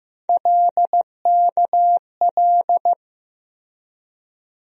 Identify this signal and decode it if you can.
Morse code, heard as a keyed tone, about 15 wpm, 700 Hz. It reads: LKL
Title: USA Call Sign Postfixes - 3 Letters 15wpm